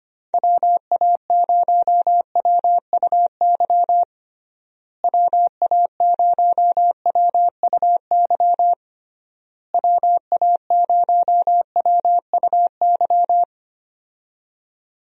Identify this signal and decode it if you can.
Morse code, heard as a keyed tone, about 25 wpm, 700 Hz. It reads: WA0WUY WA0WUY WA0WUY
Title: USA Call Signs - 6 Characters - Repeated 3x - 25wpm